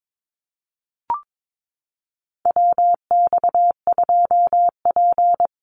Morse code, heard as a keyed tone, about 22 wpm, 700 Hz. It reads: WX2P
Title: USA Call Signs Repeated 3x 22wpm